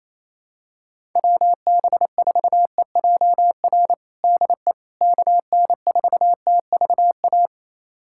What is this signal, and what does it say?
Morse code, heard as a keyed tone, about 28 wpm, 700 Hz. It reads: WB4EJR DE KN4TVA
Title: USA Callsign DE Callsign 28wpm